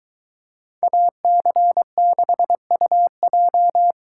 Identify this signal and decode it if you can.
Morse code, heard as a keyed tone, about 25 wpm, 700 Hz. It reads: AC6UJ